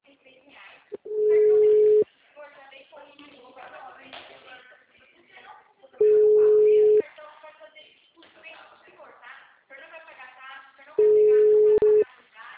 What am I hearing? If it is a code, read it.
Ringback tone (Europe, 425 Hz), with speech.
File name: out- -2003- - - .31920